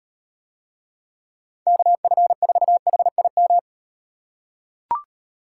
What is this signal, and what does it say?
Morse code, heard as a keyed tone, about 38 wpm, 700 Hz. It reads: KF4HIM